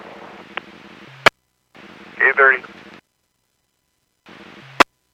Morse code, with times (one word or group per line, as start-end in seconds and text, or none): none